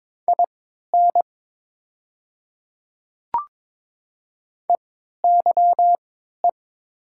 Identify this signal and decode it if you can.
Morse code, heard as a keyed tone, about 22 wpm, spaced as for 12 wpm, 700 Hz. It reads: IN EYE
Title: 3x Character Spacing - Top 300 Words - 22wpm